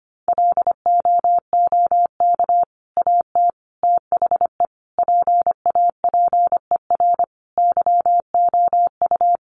Morse code, heard as a keyed tone, about 25 wpm, 700 Hz. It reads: LOOK AT THE PAPER YOU
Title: Sentences from Top 400 Words 25wpm